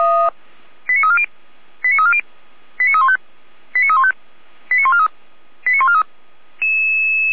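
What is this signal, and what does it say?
Five-tone selective call (ZVEI), heard as quick runs of five tones, tones of about 70 ms. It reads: 89340 89340 89315 89315 89143 89143
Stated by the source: ZVEI five-tone selective call 89280, received on an SDR.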